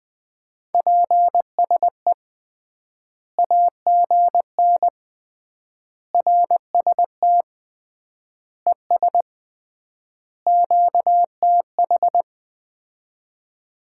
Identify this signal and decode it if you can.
Morse code, heard as a keyed tone, about 21 wpm, 700 Hz. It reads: PSE AGN RST ES QTH
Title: Common QSO Phrases - 3x Word-Spacing - 20wpm